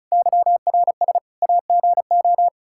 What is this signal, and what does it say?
Morse code, heard as a keyed tone, about 36 wpm, 700 Hz. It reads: YRS AGO
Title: Practice QSO - Exchange 3 - More Info - 35wpm